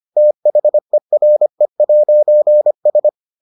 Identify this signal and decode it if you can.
Morse code, heard as a keyed tone, about 25 wpm, 600 Hz. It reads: THERE'S